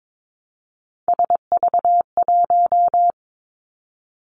Morse code, heard as a keyed tone, about 22 wpm, 700 Hz. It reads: SV1
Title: Callsign Prefixes for Top 50 DXCC - 22wpm